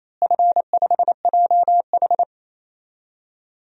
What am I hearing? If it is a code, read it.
Morse code, heard as a keyed tone, about 27 wpm, 700 Hz. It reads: F5JH
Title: DXCC-100 Call Signs - 4 Characters - Repeated 2x - 28wpm